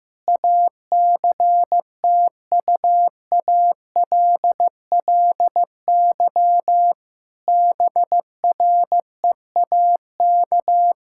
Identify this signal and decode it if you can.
Morse code, heard as a keyed tone, about 15 wpm, 700 Hz. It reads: ACTUALLY BREAK
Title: Sentences from Top 1000 Words - Review Part 6 - 15wpm